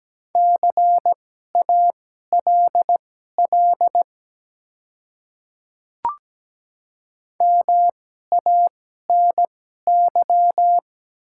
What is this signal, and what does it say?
Morse code, heard as a keyed tone, about 17 wpm, spaced as for 12 wpm, 700 Hz. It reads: CALL MANY
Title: 2x Character Spacing - Top 100 Words - 17wpm